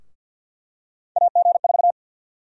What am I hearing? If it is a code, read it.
Morse code, heard as a keyed tone, about 48 wpm, 700 Hz. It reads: AG4
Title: USA Call Sign Prefixes - Repeated 2x - 50wpm